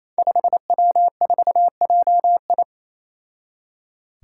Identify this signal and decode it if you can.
Morse code, heard as a keyed tone, about 28 wpm, 700 Hz. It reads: 5W4JI
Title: Five Challenging Letter-Number Combinations 28wpm